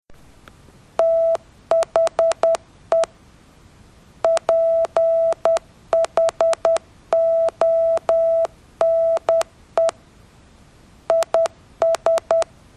Morse code, heard as a keyed tone, about 10 wpm, 660 Hz. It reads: THE PHONE IS